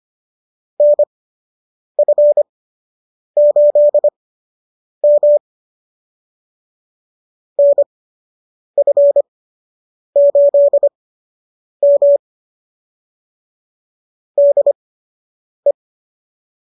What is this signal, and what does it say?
Morse code, heard as a keyed tone, about 25 wpm, spaced as for 8 wpm, 590 Hz. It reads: NF8M NF8M DE